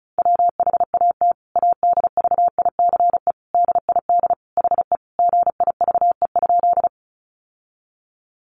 Morse code, heard as a keyed tone, about 34 wpm, 700 Hz. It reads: WHAT ADVICE DID HE GIVE?
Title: Sentences from Top 1100 Words 35wpm